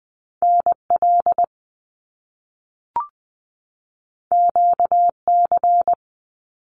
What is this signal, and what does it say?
Morse code, heard as a keyed tone, about 20 wpm, 700 Hz. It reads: NL QC
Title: Canadian Provinces and Territories 20wpm